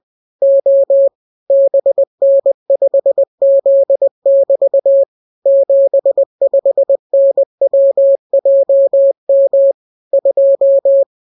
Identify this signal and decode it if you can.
Morse code, heard as a keyed tone, about 20 wpm, 550 Hz. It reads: O BN5Z<BT> 75NWJM 2